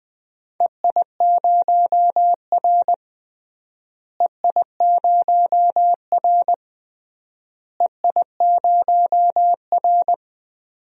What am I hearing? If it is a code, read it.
Morse code, heard as a keyed tone, about 20 wpm, 700 Hz. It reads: EI0R EI0R EI0R
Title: DXCC-100 Call Signs - 4 Characters - Repeated 3x - 20wpm